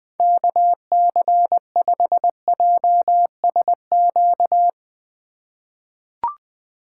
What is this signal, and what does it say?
Morse code, heard as a keyed tone, about 20 wpm, 700 Hz. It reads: KC5JSQ
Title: USA Call Signs - 3x Character Spacing - 20wpm